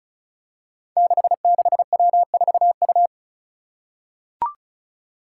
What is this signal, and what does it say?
Morse code, heard as a keyed tone, about 36 wpm, 700 Hz. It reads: 66W4U
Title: Five Challenging Letter-Number Combinations 35wpm